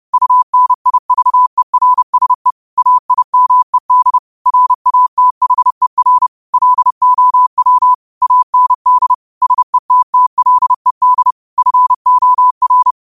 Morse code, heard as a keyed tone, about 30 wpm, 1000 Hz. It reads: UNIVERSE AIMED RATHER LOW AND SETTLED FOR